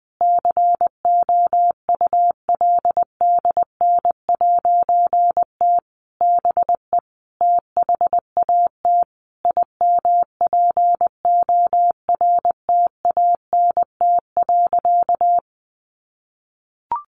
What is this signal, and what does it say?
Morse code, heard as a keyed tone, about 19 wpm, 700 Hz. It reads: COULDN'T BE THAT IMPORTANT.
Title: A Sound Of Thunder 20wpm